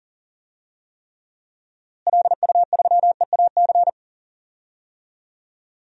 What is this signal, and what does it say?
Morse code, heard as a keyed tone, about 40 wpm, 700 Hz. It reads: LU3EAC
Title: DXCC-100 Call Signs - 6 Characters - Repeated 2x - 40wpm